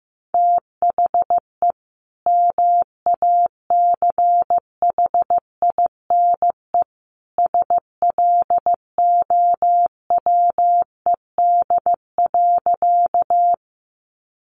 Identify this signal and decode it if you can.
Morse code, heard as a keyed tone, about 15 wpm, 700 Hz. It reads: THE MACHINE SLOWED.
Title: A Sound Of Thunder 15wpm